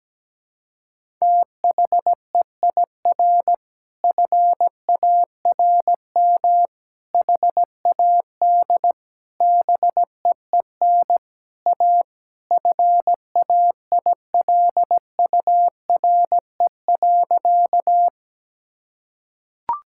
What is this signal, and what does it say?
Morse code, heard as a keyed tone, about 17 wpm, 700 Hz. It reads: THEIR FARM HAD BEEN A FAILURE.
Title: Sentences from Top 1000 Words - Review Part 6 - 17wpm